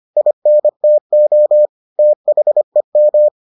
Morse code, heard as a keyed tone, about 25 wpm, 600 Hz. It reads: INTO THEM